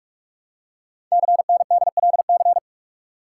Morse code, heard as a keyed tone, about 44 wpm, 700 Hz. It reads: CNDLC
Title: Code Groups - 5 Letters - 45wpm